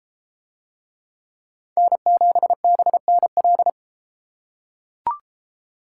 Morse code, heard as a keyed tone, about 32 wpm, 700 Hz. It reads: N7BNL